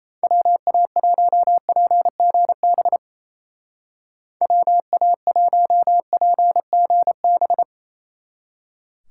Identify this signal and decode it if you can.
Morse code, heard as a keyed tone, about 34 wpm, 700 Hz. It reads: WA1PGB WA1PGB
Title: Speed Racing - USA 2x3 Call Signs 28wpm